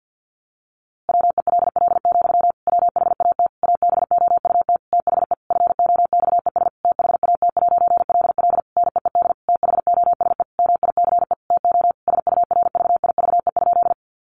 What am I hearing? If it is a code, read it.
Morse code, heard as a keyed tone, about 50 wpm, 700 Hz. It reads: WELL, WHAT ABOUT THE FOXES THAT'LL NEED THOSE MICE TO SURVIVE?
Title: A Sound Of Thunder 50wpm